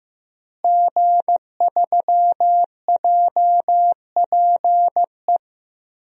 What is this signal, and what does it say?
Morse code, heard as a keyed tone, about 15 wpm, 700 Hz. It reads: G3JPE
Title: DXCC-100 Call Signs - 5 Characters - Repeated 3x - 15wpm